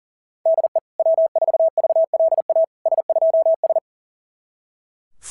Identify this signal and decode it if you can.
Morse code, heard as a keyed tone, about 40 wpm, 660 Hz. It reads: DE W4VLA S2S
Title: SOTA QSO Practice 40wpm